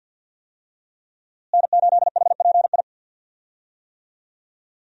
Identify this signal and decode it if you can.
Morse code, heard as a keyed tone, about 52 wpm, 700 Hz. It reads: N8HPI
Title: USA Call Signs - 5 Characters - Repeated 2x - 50wpm